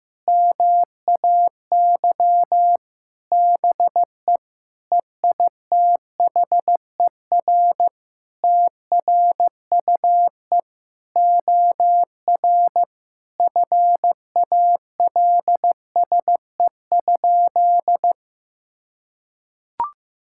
Morse code, heard as a keyed tone, about 15 wpm, 700 Hz. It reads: MAY BE EITHER TRUE OR FALSE?
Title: Sentences from Top 1600 Words 15wpm